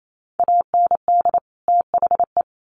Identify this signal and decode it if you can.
Morse code, heard as a keyed tone, about 28 wpm, 700 Hz. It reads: AND THE
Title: Sets of 2 Words 28wpm